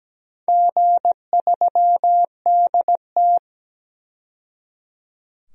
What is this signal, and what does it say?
Morse code, heard as a keyed tone, about 17 wpm, 700 Hz. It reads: G3DT
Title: DXCC-100 Call Signs - 4 Characters - 17wpm